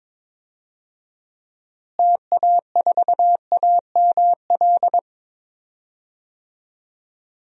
Morse code, heard as a keyed tone, about 22 wpm, 700 Hz. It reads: TA4AML
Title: DXCC-100 Call Signs - 6 Characters - Repeated 3x - 22wpm